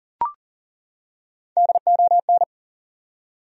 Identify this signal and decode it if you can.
Morse code, heard as a keyed tone, about 40 wpm, 700 Hz. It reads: DON
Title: USA Common Names - Max 3 Letters - 40wpm